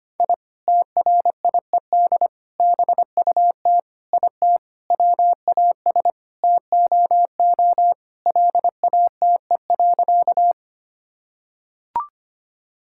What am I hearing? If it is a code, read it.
Morse code, heard as a keyed tone, about 25 wpm, 700 Hz. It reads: I TRIED BUT IT WAS TOO LATE.
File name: practice / 25wpm / Sentences with 3-Word Phrases - 25wpm